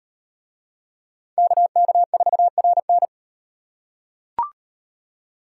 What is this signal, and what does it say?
Morse code, heard as a keyed tone, about 37 wpm, 700 Hz. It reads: KK4RN